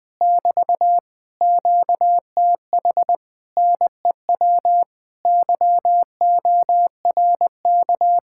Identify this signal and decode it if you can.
Morse code, heard as a keyed tone, about 20 wpm, 700 Hz. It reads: <BT> QTH NEW YORK
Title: Speed-Racing Common QSO Phrases Encore - 17wpm